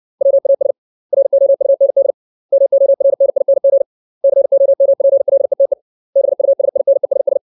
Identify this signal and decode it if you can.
Morse code, heard as a keyed tone, about 60 wpm, 550 Hz. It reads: WAS ROUND MORNING COMPLETE BUSINESS
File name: 632 words / 632 Words 60WPM